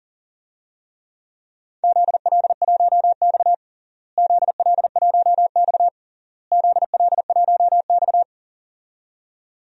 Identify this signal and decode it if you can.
Morse code, heard as a keyed tone, about 40 wpm, 700 Hz. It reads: ZL1X ZL1X ZL1X
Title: DXCC-100 Call Signs - 4 Characters - Repeated 3x - 40wpm